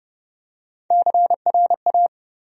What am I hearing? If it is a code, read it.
Morse code, heard as a keyed tone, about 30 wpm, 700 Hz. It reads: CRA
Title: USA Call Sign Postfixes - 3 Letters 30wpm